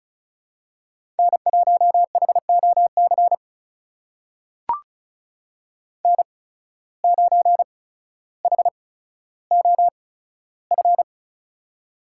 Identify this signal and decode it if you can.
Morse code, heard as a keyed tone, about 36 wpm, spaced as for 15 wpm, 700 Hz. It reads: N1HOC N9HOF
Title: USA Call Signs - 5 Characters - 8x Character Spacing - 35wpm